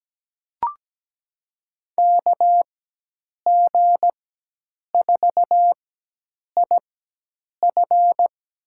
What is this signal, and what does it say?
Morse code, heard as a keyed tone, about 17 wpm, spaced as for 8 wpm, 700 Hz. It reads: KG4IF
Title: USA Call Signs - 4x Character Spacing - 17wpm